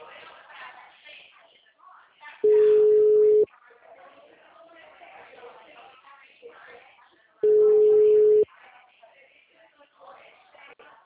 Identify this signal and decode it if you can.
Ringback tone (Europe, 425 Hz), with speech.